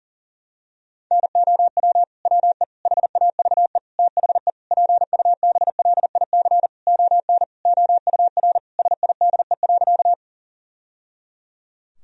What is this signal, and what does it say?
Morse code, heard as a keyed tone, about 40 wpm, 700 Hz. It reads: NOW WE HAVE THE PUBLIC ON OUR SIDE.
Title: Sentences from Top 1000 Words - Review Part 3 - 40wpm